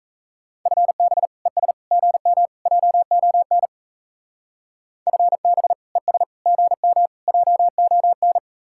Morse code, heard as a keyed tone, about 32 wpm, 700 Hz. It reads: FB ES GM JON FB ES GM JON
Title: Speed-Racing Common QSO Phrases - 28wpm